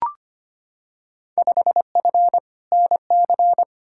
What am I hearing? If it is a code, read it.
Morse code, heard as a keyed tone, about 25 wpm, 700 Hz. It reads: 5F NC
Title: Field Day Exchanges Encore - 25wpm